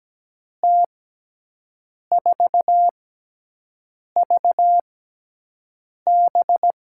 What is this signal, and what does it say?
Morse code, heard as a keyed tone, about 17 wpm, spaced as for 6 wpm, 700 Hz. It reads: T4VB
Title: USA Call Signs - 6x Character Spacing - 17wpm